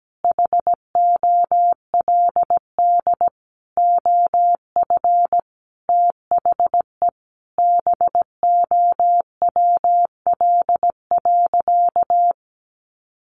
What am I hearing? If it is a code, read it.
Morse code, heard as a keyed tone, about 17 wpm, 700 Hz. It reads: HOLD OF THE BOWL.
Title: Sentences from Top 1200 Words 17wpm